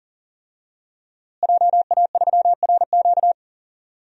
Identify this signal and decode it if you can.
Morse code, heard as a keyed tone, about 39 wpm, 700 Hz. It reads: JA3RQ
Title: DXCC-100 Call Signs - 5 Characters - Repeated 3x - 40wpm